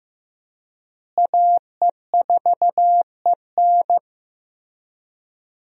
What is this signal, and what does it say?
Morse code, heard as a keyed tone, about 15 wpm, 700 Hz. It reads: AE4EN
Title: USA Call Signs - 5 Characters - Repeated 3x - 15wpm